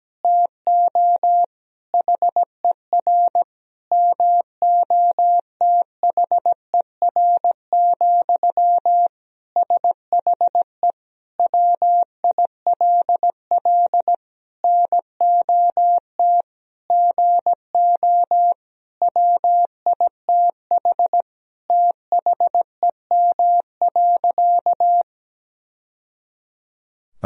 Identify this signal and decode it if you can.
Morse code, heard as a keyed tone, about 17 wpm, 700 Hz. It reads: TO HER MOTHER, SHE WILL NOT GO WITH THEM.
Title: Sentences from Top 1000 Words - Review Part 4 - 17wpm